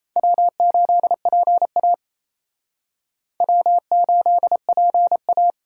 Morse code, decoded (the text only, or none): W8PA W8PA